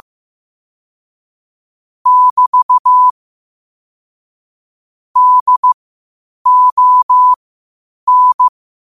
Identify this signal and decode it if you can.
Morse code, heard as a keyed tone, about 15 wpm, spaced as for 8 wpm, 1000 Hz. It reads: <BT> DON
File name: qotdfiles / 8wpm